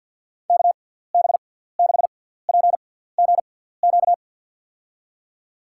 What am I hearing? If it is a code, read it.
Morse code, heard as a keyed tone, about 47 wpm, spaced as for 17 wpm, 700 Hz. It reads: KB6PGQ
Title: USA Call Signs - 6 Characters - 6x Character Spacing - 50wpm